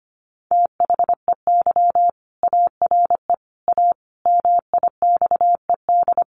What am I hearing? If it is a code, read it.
Morse code, heard as a keyed tone, about 25 wpm, 700 Hz. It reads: THEY ARE A MIXED
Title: Sentences from Top 1500 Words 25wpm